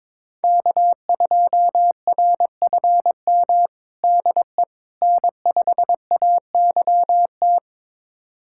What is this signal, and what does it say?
Morse code, heard as a keyed tone, about 22 wpm, 700 Hz. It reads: K2RFM DE N5AYT
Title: USA Callsign DE Callsign - 5 Characters - 22wpm